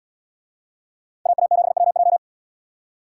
Morse code, heard as a keyed tone, about 73 wpm, 700 Hz. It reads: AA7WJ